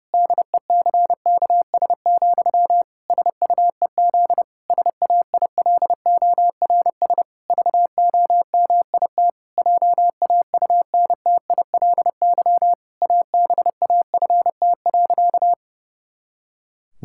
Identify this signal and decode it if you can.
Morse code, heard as a keyed tone, about 30 wpm, 700 Hz. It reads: DECKS, SUEZ SAILORS VOMIT JAUNTILY ABAFT.
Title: Pangram Challenge v2 - 30wpm